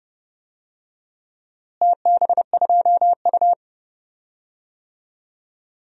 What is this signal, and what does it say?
Morse code, heard as a keyed tone, about 32 wpm, 700 Hz. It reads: TB2U
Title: DXCC-100 Call Signs - 4 Characters - Repeated 2x - 30wpm